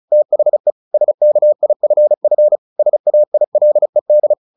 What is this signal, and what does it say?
Morse code, heard as a keyed tone, about 34 wpm, 600 Hz. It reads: THE SKIFF SAILED
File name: the old man and the sea 2 35wpm